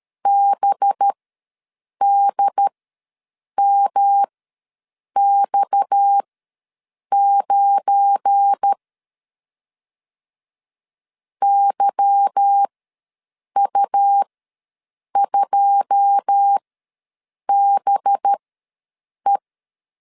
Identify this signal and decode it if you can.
Morse code, heard as a keyed tone, about 13 wpm, spaced as for 7 wpm, 800 Hz. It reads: BDMX9 YU2BE
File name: LEZIONI / 9 2 ws 12wpm 3406